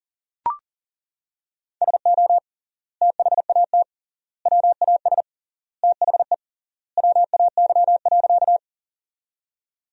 Morse code, heard as a keyed tone, about 40 wpm, 700 Hz. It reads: SO THAT WAS THE WAY.
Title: 3x Word-Spacing - Sentences from Top 100 Words - 40wpm